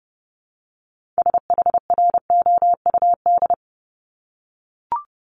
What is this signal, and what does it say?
Morse code, heard as a keyed tone, about 30 wpm, 700 Hz. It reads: SHROUD
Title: Multi-letter Phonemes 30wpm